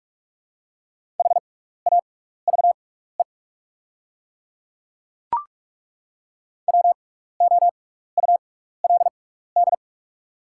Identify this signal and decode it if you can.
Morse code, heard as a keyed tone, about 48 wpm, spaced as for 16 wpm, 690 Hz. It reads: HAVE WOULD
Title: 6x Character Spacing - Top 200 Words - 45wpm